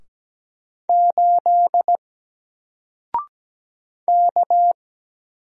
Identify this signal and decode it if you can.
Morse code, heard as a keyed tone, about 17 wpm, 700 Hz. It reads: 8 K
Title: Single Letter-Number Alt 17wpm